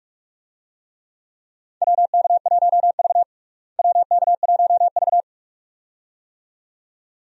Morse code, heard as a keyed tone, about 42 wpm, 700 Hz. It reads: WK1V WK1V
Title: USA Call Signs - 4 Characters - Repeated 2x - 45wpm